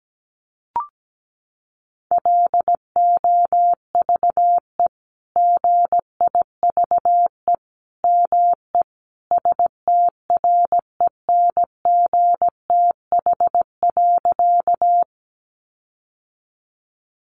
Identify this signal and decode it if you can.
Morse code, heard as a keyed tone, about 17 wpm, 700 Hz. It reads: LOVE GIVE ME STRENGTH.
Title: Sentences from Top 1300 Words 17wpm